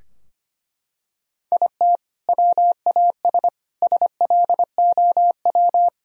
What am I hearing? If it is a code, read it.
Morse code, heard as a keyed tone, about 25 wpm, 700 Hz. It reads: IT WAS SLOW
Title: Sentences from Top 1000 Words - Review Part 2 - 25wpm